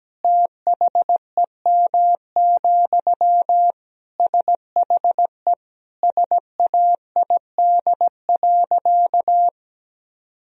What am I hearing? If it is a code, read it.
Morse code, heard as a keyed tone, about 17 wpm, 700 Hz. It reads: THEM, SHE SAID.
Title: Sentences from Top 1000 Words - Review Part 2 - 17wpm